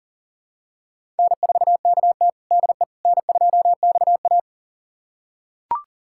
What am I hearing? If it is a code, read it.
Morse code, heard as a keyed tone, about 40 wpm, 700 Hz. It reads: N4KT DE N2XA
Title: USA Callsign DE Callsign - 4 Characters - 40wpm